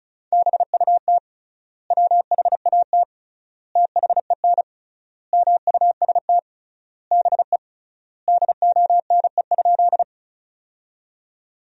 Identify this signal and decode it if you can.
Morse code, heard as a keyed tone, about 34 wpm, 700 Hz. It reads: BUT WHAT THEN MUST BE DONE?
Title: 3x Word-Spacing - Sentences from Top 500 Words - 35wpm